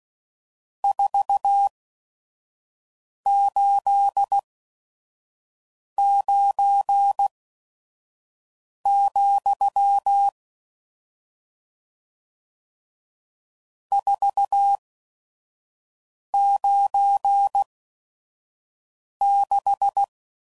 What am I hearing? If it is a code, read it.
Morse code, heard as a keyed tone, about 16 wpm, spaced as for 5 wpm, 800 Hz. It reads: 489, 496
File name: CW Code Groups @5WPM / RST's / Various RST No's